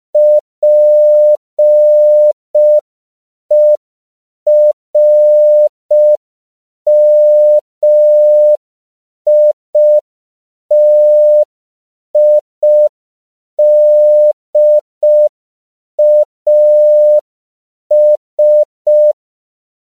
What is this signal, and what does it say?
Morse code, heard as a keyed tone, about 5 wpm, 600 Hz. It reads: PERMITIDAS
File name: CW / 5PPM